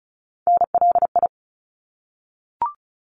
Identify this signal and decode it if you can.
Morse code, heard as a keyed tone, about 34 wpm, 700 Hz. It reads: NLI
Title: ARRL Sections 35wpm